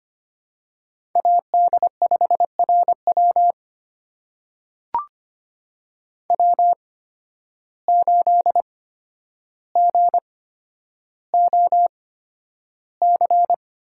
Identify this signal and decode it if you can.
Morse code, heard as a keyed tone, about 25 wpm, spaced as for 11 wpm, 700 Hz. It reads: AD5RW W8GOC
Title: USA Call Signs - 5 Characters - 8x Character Spacing - 25wpm